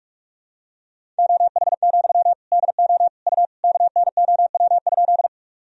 Morse code, heard as a keyed tone, about 41 wpm, 700 Hz. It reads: OH, DO U KNOW?
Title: Sentences from Top 500 Words and Abbreviations 45wpm